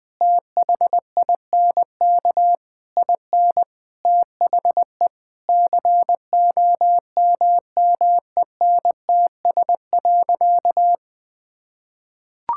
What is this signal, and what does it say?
Morse code, heard as a keyed tone, about 20 wpm, 700 Hz. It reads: THINK IN THE COMMENTS.
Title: Sentences with 8-Word Phrases - 20wpm